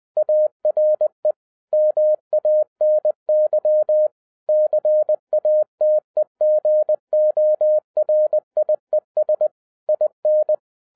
Morse code, heard as a keyed tone, about 20 wpm, 610 Hz. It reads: ARE MANY CATEGORIES IN
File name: 2011-10-mp3 / n13-TIPS FROM MY 34 YEARS OF EXPERIENCE 20wpm